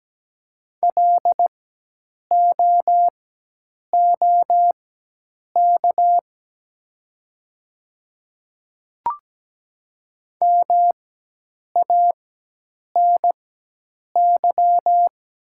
Morse code, heard as a keyed tone, about 17 wpm, spaced as for 8 wpm, 700 Hz. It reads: LOOK MANY
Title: 4x Character Spacing - Top 100 Words - 17wpm